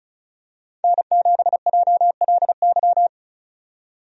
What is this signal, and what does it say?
Morse code, heard as a keyed tone, about 36 wpm, 700 Hz. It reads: N7JLY
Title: USA Call Signs - 5 Characters - 35wpm